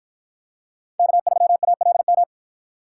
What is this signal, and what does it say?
Morse code, heard as a keyed tone, about 53 wpm, 700 Hz. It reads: K3ALM